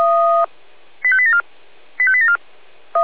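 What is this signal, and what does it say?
Five-tone selective call (ZVEI), heard as quick runs of five tones, tones of about 70 ms. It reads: 85874 85874
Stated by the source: Five-tone ZVEI selective call 85870 received by SDR.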